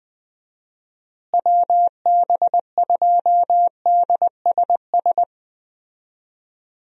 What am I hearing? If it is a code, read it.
Morse code, heard as a keyed tone, about 20 wpm, 700 Hz. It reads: WB2DSS
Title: USA Call Signs - 6 Characters - Repeated 2x - 20wpm